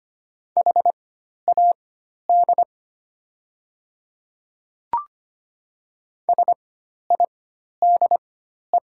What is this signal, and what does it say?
Morse code, heard as a keyed tone, about 25 wpm, spaced as for 12 wpm, 700 Hz. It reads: HAD SIDE
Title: 4x Character Spacing - Top 100 Words - 25wpm